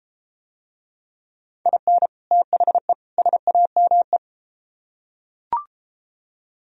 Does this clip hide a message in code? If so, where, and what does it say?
Yes: Morse code from 1.7 to 4.2 s, heard as a keyed tone, about 33 wpm, 700 Hz. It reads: IN THE SAME